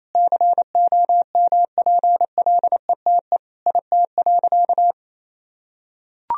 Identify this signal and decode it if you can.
Morse code, heard as a keyed tone, about 27 wpm, 700 Hz. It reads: COMPLETE IT.
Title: Sentences from Top 1000 Words - Review Part 3 - 28wpm